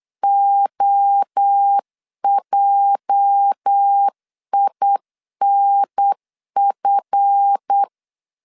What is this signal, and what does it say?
Morse code, heard as a keyed tone, about 8 wpm, 800 Hz. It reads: OJINF
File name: CW / DET / 1 9 1500C S A 8wpm 5637